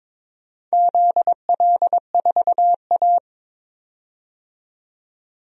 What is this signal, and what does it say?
Morse code, heard as a keyed tone, about 22 wpm, 700 Hz. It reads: ZL4A
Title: DXCC-100 Call Signs - 4 Characters - 22wpm